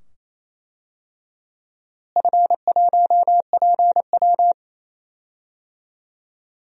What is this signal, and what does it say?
Morse code, heard as a keyed tone, about 28 wpm, 700 Hz. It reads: F1PW
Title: DXCC-100 Call Signs - 4 Characters - Repeated 2x - 28wpm